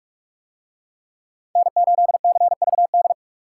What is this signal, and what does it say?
Morse code, heard as a keyed tone, about 43 wpm, 700 Hz. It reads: N8CVD